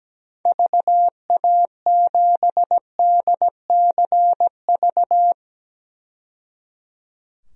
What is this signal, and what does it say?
Morse code, heard as a keyed tone, about 17 wpm, 700 Hz. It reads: VA7DCV
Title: DXCC-100 Call Signs - 6 Characters - 17wpm